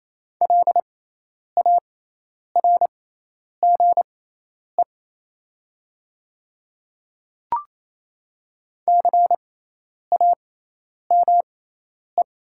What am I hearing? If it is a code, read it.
Morse code, heard as a keyed tone, about 28 wpm, spaced as for 10 wpm, 700 Hz. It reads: LARGE CAME
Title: 6x Character Spacing - Top 300 Words - 28wpm